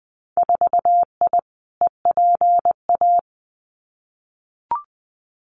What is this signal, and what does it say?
Morse code, heard as a keyed tone, about 20 wpm, 700 Hz. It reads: 4I EPA
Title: Winter Field Day Exchanges Encore 20wpm